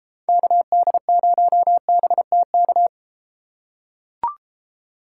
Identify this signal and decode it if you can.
Morse code, heard as a keyed tone, about 31 wpm, 700 Hz. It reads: KD0BTK